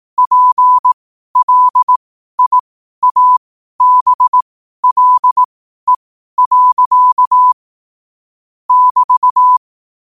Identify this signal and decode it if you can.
Morse code, heard as a keyed tone, about 18 wpm, spaced as for 12 wpm, 1000 Hz. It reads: PLIABLE. <BT>
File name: qotdfiles / 12wpm